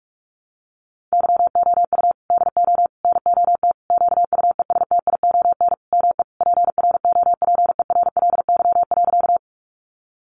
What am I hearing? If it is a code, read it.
Morse code, heard as a keyed tone, about 44 wpm, 700 Hz. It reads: YOU DO NOT QUESTION ME PROPERLY.
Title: Sentences from Top 1100 Words 45wpm